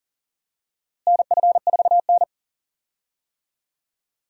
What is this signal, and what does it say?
Morse code, heard as a keyed tone, about 40 wpm, 700 Hz. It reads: NF4N
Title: USA Call Signs - 4 Characters - Repeated 2x - 40wpm